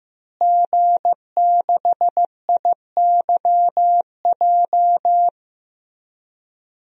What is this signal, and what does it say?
Morse code, heard as a keyed tone, about 15 wpm, 700 Hz. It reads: G6IYJ
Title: DXCC-100 Call Signs - 5 Characters - Repeated 3x - 15wpm